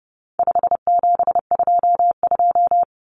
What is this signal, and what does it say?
Morse code, heard as a keyed tone, about 30 wpm, 700 Hz. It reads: 5722
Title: Common Cut Numbers - 4-Digit 30wpm